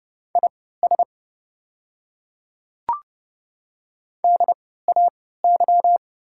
Morse code, heard as a keyed tone, about 30 wpm, spaced as for 17 wpm, 700 Hz. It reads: IS DAY
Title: 3x Character Spacing - Top 100 Words - 30wpm